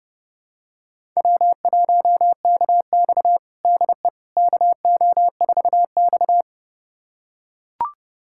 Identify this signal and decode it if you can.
Morse code, heard as a keyed tone, about 32 wpm, 700 Hz. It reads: W1KX DE KO4X
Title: USA Callsign DE Callsign - 4 Characters - 30wpm